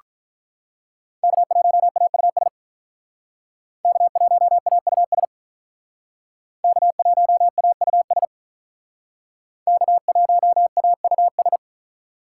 Morse code, heard as a keyed tone, about 43 wpm, 700 Hz. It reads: K1AUS K1AUS K1AUS K1AUS
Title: Speed Racing - USA 1x3 Call Signs 35wpm